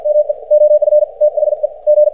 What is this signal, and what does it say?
Morse code, heard as a keyed tone, about 57 wpm, 590 Hz. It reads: ME OF THE G
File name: archived / CRSnet